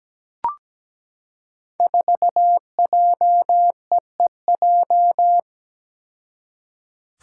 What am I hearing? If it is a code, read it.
Morse code, heard as a keyed tone, about 17 wpm, 700 Hz. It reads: 4JEEJ
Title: Five Challenging Letter-Number Combinations 17wpm